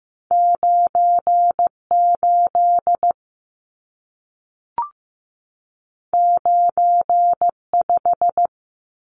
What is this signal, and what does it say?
Morse code, heard as a keyed tone, about 15 wpm, 700 Hz. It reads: 98 95
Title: Numbers - 2-Digit 15wpm